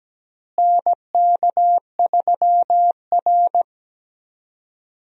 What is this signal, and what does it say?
Morse code, heard as a keyed tone, about 17 wpm, 700 Hz. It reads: NK3R
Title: USA Call Signs - 4 Characters - Repeated 3x - 17wpm